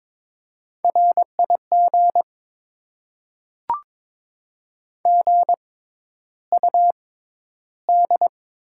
Morse code, heard as a keyed tone, about 22 wpm, spaced as for 11 wpm, 700 Hz. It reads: RIG GUD
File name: Top 20 QSO Elements - 6x Character Spacing - 22wpm